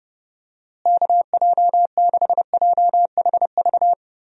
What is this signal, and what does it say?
Morse code, heard as a keyed tone, about 30 wpm, 700 Hz. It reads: KJ6JHV
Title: USA Call Signs - 6 Characters - Repeated 2x - 30wpm